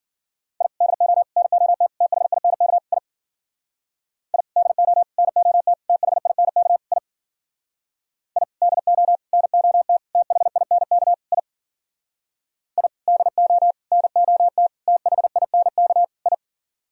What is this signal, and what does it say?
Morse code, heard as a keyed tone, about 54 wpm, 700 Hz. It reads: I DO NOT THINK I I DO NOT THINK I I DO NOT THINK I I DO NOT THINK I